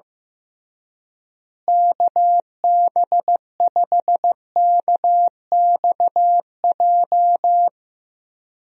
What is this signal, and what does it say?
Morse code, heard as a keyed tone, about 16 wpm, 700 Hz. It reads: KB5KXJ
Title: USA Call Signs - 6 Characters - Repeated 3x - 15wpm